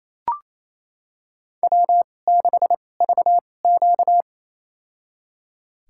Morse code, heard as a keyed tone, about 28 wpm, spaced as for 20 wpm, 700 Hz. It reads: W6VQ
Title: USA Call Signs - 4 Characters - 2x Character Spacing - 28wpm